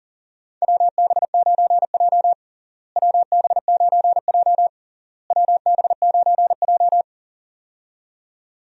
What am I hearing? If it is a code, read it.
Morse code, heard as a keyed tone, about 40 wpm, 700 Hz. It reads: WB9J WB9J WB9J
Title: USA Call Signs - 4 Characters - Repeated 3x - 40wpm